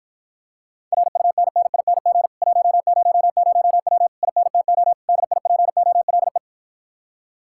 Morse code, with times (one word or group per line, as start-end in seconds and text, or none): 0.9-2.3 RUNNING
2.4-4.1 100W
4.2-4.9 INTO
5.1-6.4 DIPOLE